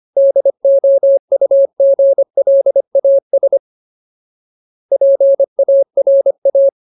Morse code, heard as a keyed tone, about 25 wpm, 550 Hz. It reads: DOUGLAS PARA
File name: Ultimate Word List / 25WPM Ultimate